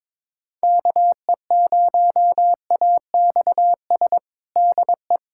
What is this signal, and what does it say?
Morse code, heard as a keyed tone, about 22 wpm, 700 Hz. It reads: KE0AXS DE
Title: USA Callsign DE Callsign 22wpm